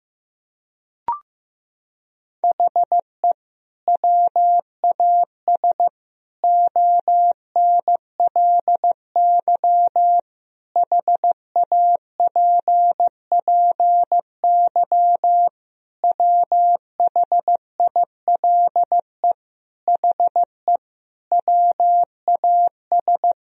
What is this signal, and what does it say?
Morse code, heard as a keyed tone, about 15 wpm, 700 Hz. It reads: HE WAS ONLY HAPPY WHILE HE WAS
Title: Sentences from Top 1000 Words - Review Part 4 - 15wpm